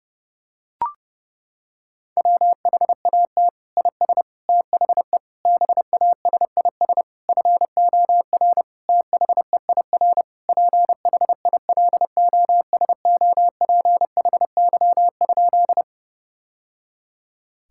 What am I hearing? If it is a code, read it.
Morse code, heard as a keyed tone, about 30 wpm, 700 Hz. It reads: WHAT IS THE BASIS FOR THEIR PHILOSOPHY?
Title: Sentences from Top 1000 Words - Review Part 6 - 30wpm